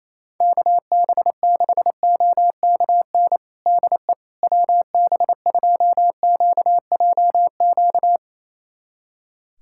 Morse code, heard as a keyed tone, about 28 wpm, 700 Hz. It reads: KB6OKN DE WB2QJQ
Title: USA Callsign DE Callsign - 6 Characters - 28wpm